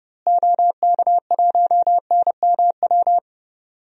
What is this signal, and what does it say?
Morse code, heard as a keyed tone, about 30 wpm, 700 Hz. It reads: OK1NMW
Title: DXCC-100 Call Signs - 6 Characters - Repeated 3x - 30wpm